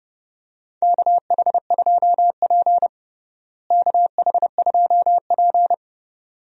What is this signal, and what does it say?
Morse code, heard as a keyed tone, about 30 wpm, 700 Hz. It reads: KH2P KH2P
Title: USA Call Signs - 4 Characters - Repeated 2x - 30wpm